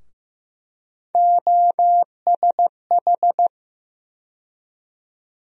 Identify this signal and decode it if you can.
Morse code, heard as a keyed tone, about 15 wpm, 700 Hz. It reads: OSH
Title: USA Call Sign Postfixes - 3 Letters 15wpm